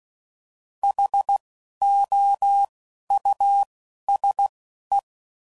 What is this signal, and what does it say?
Morse code, heard as a keyed tone, about 16 wpm, spaced as for 11 wpm, 800 Hz. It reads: HOUSE
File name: Suffixes / General / Houses